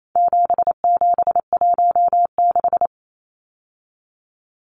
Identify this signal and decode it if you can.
Morse code, heard as a keyed tone, about 28 wpm, 700 Hz. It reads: 7716
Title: Numbers - 4-Digit 28wpm